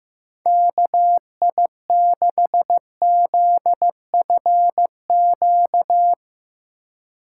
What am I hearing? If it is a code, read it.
Morse code, heard as a keyed tone, about 15 wpm, 700 Hz. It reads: KI6ZFQ
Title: Speed Racing - USA 2x3 Call Signs 15wpm